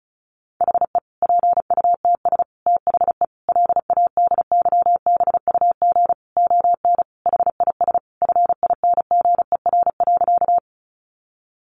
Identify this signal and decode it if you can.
Morse code, heard as a keyed tone, about 34 wpm, 700 Hz. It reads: HE PUTS THE LADYBUG ON HIS FINGER.
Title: Additional Easy Phrases 35wpm